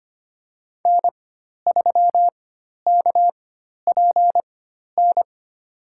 Morse code, heard as a keyed tone, about 25 wpm, spaced as for 12 wpm, 700 Hz. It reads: N3KPN
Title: USA Call Signs - 5 Characters - 4x Character Spacing - 25wpm